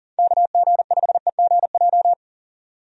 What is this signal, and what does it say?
Morse code, heard as a keyed tone, about 38 wpm, 700 Hz. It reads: KG5EGJ
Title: USA Call Signs - 6 Characters - 40wpm